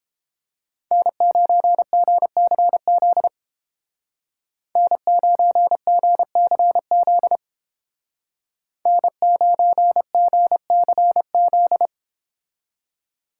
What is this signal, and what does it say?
Morse code, heard as a keyed tone, about 29 wpm, 700 Hz. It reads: N9GCZ N9GCZ N9GCZ